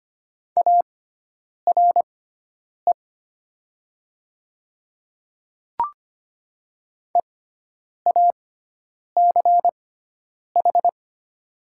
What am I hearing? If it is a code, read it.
Morse code, heard as a keyed tone, about 25 wpm, spaced as for 9 wpm, 700 Hz. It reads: ARE EACH
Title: 6x Character Spacing - Top 100 Words - 25wpm